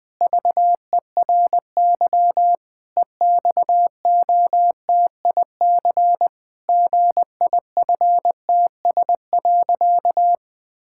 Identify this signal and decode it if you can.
Morse code, heard as a keyed tone, about 21 wpm, 700 Hz. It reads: VERY EXOTIC GIFTS.
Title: Pangram Challenge v2 - 20wpm